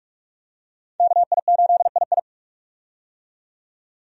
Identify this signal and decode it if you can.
Morse code, heard as a keyed tone, about 45 wpm, 700 Hz. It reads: KI8II
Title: USA Call Signs - 5 Characters - 45wpm